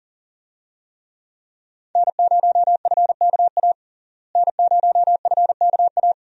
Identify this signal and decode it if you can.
Morse code, heard as a keyed tone, about 40 wpm, 700 Hz. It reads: N0FKA N0FKA
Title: USA Call Signs - 5 Characters - Repeated 3x - 40wpm